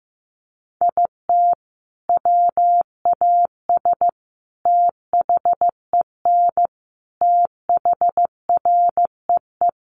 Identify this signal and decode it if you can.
Morse code, heard as a keyed tone, about 15 wpm, 700 Hz. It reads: IT WAS THEN THREE
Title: Sentences from Top 1500 Words 15wpm